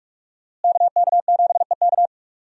Morse code, heard as a keyed tone, about 45 wpm, 700 Hz. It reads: KK7EK
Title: USA Call Signs - 5 Characters - Repeated 2x - 45wpm